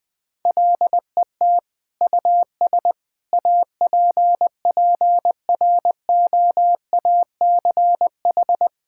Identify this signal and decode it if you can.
Morse code, heard as a keyed tone, about 20 wpm, 700 Hz. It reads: LET US APPROACH
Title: Sentences from Top 1000 Words - Review Part 6 - 20wpm